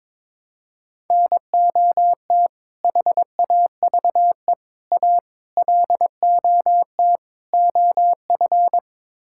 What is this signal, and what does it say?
Morse code, heard as a keyed tone, about 21 wpm, 700 Hz. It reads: NOT HAVE A LOT OF
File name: practice / 22wpm / Build A Phrase - 5-Word Phrases - 22wpm